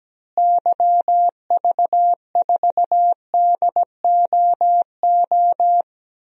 Morse code, heard as a keyed tone, about 17 wpm, 700 Hz. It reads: YV4DOO
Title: DXCC-100 Call Signs - 6 Characters - Repeated 3x - 17wpm